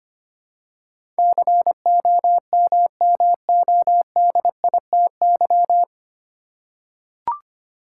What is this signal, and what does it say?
Morse code, heard as a keyed tone, about 25 wpm, 700 Hz. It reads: COMMODITY
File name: practice / 25wpm / 9 Letter Words - First 3 Letters Given - 25wpm